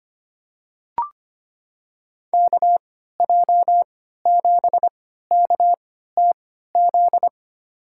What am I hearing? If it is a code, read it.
Morse code, heard as a keyed tone, about 25 wpm, spaced as for 14 wpm, 700 Hz. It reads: KJ7KTZ
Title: USA Call Signs - 6 Characters - 3x Character Spacing - 25wpm